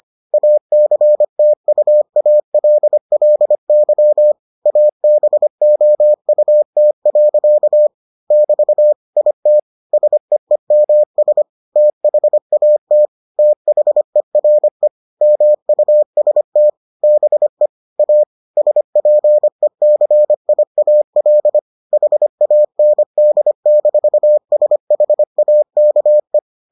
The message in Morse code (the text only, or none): ACTUALLY ABOUT. <BT> IT SEEMS THAT THERE MUST BE A SPECIAL HAND-SHAKE